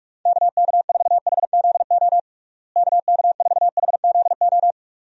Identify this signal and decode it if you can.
Morse code, heard as a keyed tone, about 46 wpm, 700 Hz. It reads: KK4HZO KK4HZO
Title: USA Call Signs - 6 Characters - Repeated 2x - 45wpm